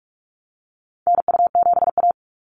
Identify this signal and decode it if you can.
Morse code, heard as a keyed tone, about 44 wpm, 700 Hz. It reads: NU7A
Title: NA SOTA Chaser Callsigns 45wpm